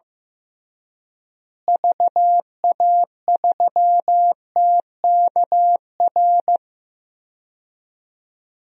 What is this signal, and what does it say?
Morse code, heard as a keyed tone, about 15 wpm, 700 Hz. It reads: VA3TKR
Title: DXCC-100 Call Signs - 6 Characters - Repeated 2x - 15wpm